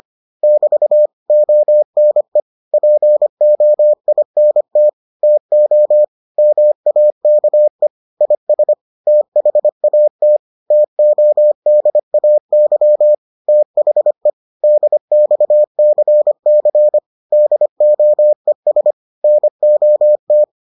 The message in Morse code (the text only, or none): <BT> ONE POINT TO MAKE IS THAT TODAY THE DXCC DOES NOT